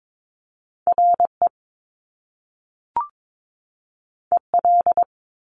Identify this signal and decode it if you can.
Morse code, heard as a keyed tone, about 22 wpm, 700 Hz. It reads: RE EL
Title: Top Two Letter Combinations in Words 22wpm